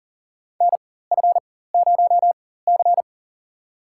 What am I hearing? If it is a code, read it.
Morse code, heard as a keyed tone, about 40 wpm, spaced as for 19 wpm, 700 Hz. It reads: NF0C
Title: USA Call Signs - 4 Characters - 4x Character Spacing - 40wpm